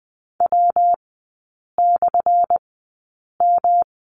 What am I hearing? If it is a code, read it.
Morse code, heard as a keyed tone, about 20 wpm, spaced as for 8 wpm, 700 Hz. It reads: W/M
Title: 178 Lesson 29e - Slash - Three Characters 20wpm